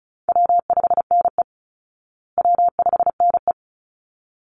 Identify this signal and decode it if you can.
Morse code, heard as a keyed tone, about 34 wpm, 700 Hz. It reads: W5NE W5NE
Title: Easy Call Signs Repeated 2x - 35wpm